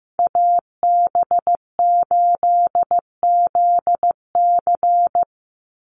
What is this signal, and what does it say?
Morse code, heard as a keyed tone, about 15 wpm, 700 Hz. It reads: AB8ZC
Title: USA Call Signs 15wpm